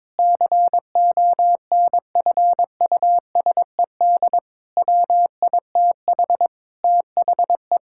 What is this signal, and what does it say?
Morse code, heard as a keyed tone, about 22 wpm, 700 Hz. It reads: CONFUSED WITH THE
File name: practice / 22wpm / Sets of 7 Words - 22wpm